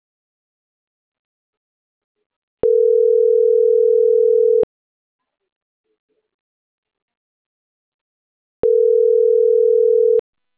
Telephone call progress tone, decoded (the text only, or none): ringback tone